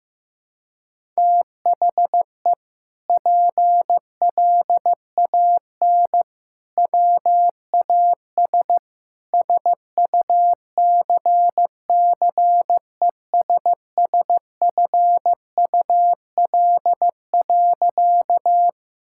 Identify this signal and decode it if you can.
Morse code, heard as a keyed tone, about 15 wpm, 700 Hz. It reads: THE PLAN WAS SUCCESSFUL.
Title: Sentences from Top 1000 Words - Review Part 6 - 15wpm